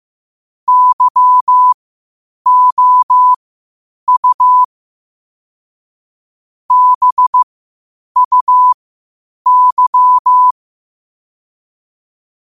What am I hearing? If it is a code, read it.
Morse code, heard as a keyed tone, about 15 wpm, spaced as for 8 wpm, 1000 Hz. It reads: YOU BUY
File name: qotdfiles / 8wpm